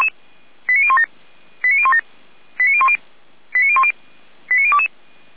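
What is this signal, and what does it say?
Five-tone selective call (ZVEI), heard as quick runs of five tones, tones of about 70 ms. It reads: 89017 89017 89010 89010 89022
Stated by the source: ZVEI five-tone selective call received on an SDR.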